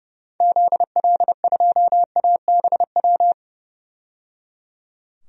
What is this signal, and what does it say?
Morse code, heard as a keyed tone, about 30 wpm, 700 Hz. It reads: ZL2ABW
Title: DXCC-100 Call Signs - 6 Characters - 30wpm